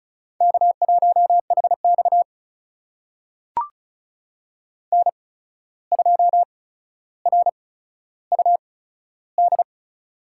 Morse code, heard as a keyed tone, about 36 wpm, spaced as for 14 wpm, 700 Hz. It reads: K1HX N2RUD
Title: USA Call Signs - 8x Character Spacing - 35wpm